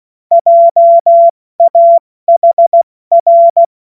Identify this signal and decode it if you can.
Morse code, heard as a keyed tone, about 16 wpm, 680 Hz. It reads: JAHR
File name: Morsekurs / Witze / Witz 24. Januar 2023 70 F80